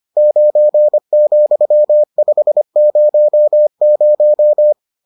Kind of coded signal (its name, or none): Morse code